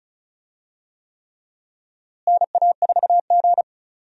Morse code, heard as a keyed tone, about 36 wpm, 700 Hz. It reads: NA4G
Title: USA Call Signs - 4 Characters - Repeated 3x - 35wpm